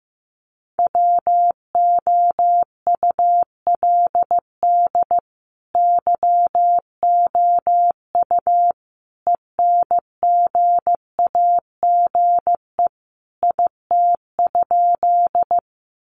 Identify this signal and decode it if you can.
Morse code, heard as a keyed tone, about 15 wpm, 700 Hz. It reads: WOULD YOU ENGAGE IT?
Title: Sentences from Top 1500 Words 15wpm